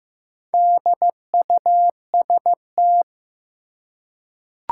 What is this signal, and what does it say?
Morse code, heard as a keyed tone, about 16 wpm, 700 Hz. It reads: DUST
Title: Four Letter Words - 15wpm